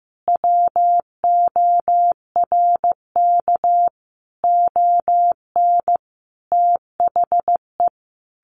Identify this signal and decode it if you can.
Morse code, heard as a keyed tone, about 15 wpm, 700 Hz. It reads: WORK ON THE
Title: Sentences from Top 1200 Words 15wpm